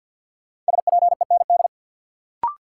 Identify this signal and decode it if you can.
Morse code, heard as a keyed tone, about 52 wpm, 700 Hz. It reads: SPEND
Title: Sets of 5 Words - New Word Drill - 50wpm